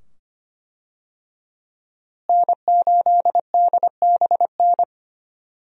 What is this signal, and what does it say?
Morse code, heard as a keyed tone, about 25 wpm, 700 Hz. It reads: N8DBN
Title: USA Call Signs - 5 Characters - Repeated 2x - 25wpm